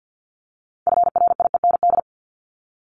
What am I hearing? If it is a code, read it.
Morse code, heard as a keyed tone, about 50 wpm, 700 Hz. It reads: FRIEND
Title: Top 300 Words 50wpm